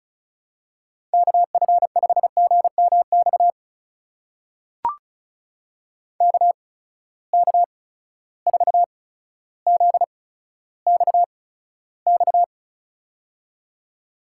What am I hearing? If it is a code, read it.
Morse code, heard as a keyed tone, about 34 wpm, spaced as for 15 wpm, 700 Hz. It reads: KF5GMX KK4ZXX
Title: USA Call Signs - 6 Characters - 8x Character Spacing - 35wpm